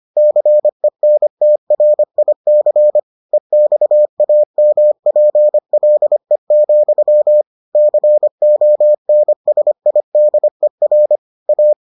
Morse code, heard as a keyed tone, about 25 wpm, 600 Hz. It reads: CENTRIC EXAMPLE, CONSIDER A